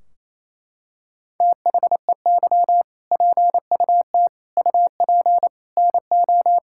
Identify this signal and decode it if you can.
Morse code, heard as a keyed tone, about 28 wpm, 700 Hz. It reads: THEY PUT UP NO
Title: Sentences from Top 1600 Words 28wpm